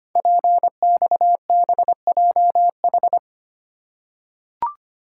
Morse code, heard as a keyed tone, about 25 wpm, 700 Hz. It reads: PXBJH
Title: Code Groups - 5 Letters - 25wpm